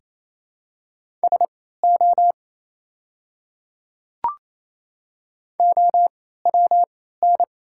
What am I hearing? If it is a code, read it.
Morse code, heard as a keyed tone, about 28 wpm, spaced as for 16 wpm, 700 Hz. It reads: SO OWN
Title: 3x Character Spacing - Top 300 Words - 28wpm